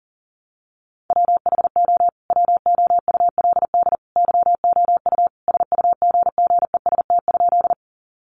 Morse code, heard as a keyed tone, about 40 wpm, 700 Hz. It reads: WHO WOULD YOU SUGGEST?
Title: Sentences from Top 1300 Words 40wpm